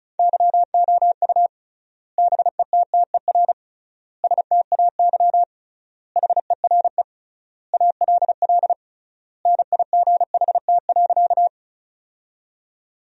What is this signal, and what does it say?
Morse code, heard as a keyed tone, about 36 wpm, 700 Hz. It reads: YOU BETTER STAY HERE ALL NIGHT.
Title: 3x Word-Spacing - Sentences from Top 500 Words - 35wpm